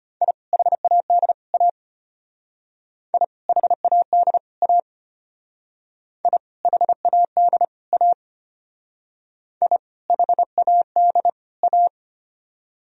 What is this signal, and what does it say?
Morse code, heard as a keyed tone, about 32 wpm, 700 Hz. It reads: I HAD A I HAD A I HAD A I HAD A